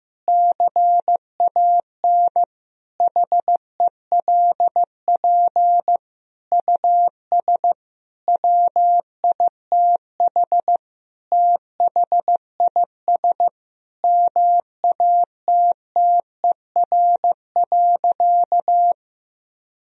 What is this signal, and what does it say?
Morse code, heard as a keyed tone, about 15 wpm, 700 Hz. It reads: CAN HELP US WITH THIS MATTER.
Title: Sentences from Top 1000 Words - Review Part 2 - 15wpm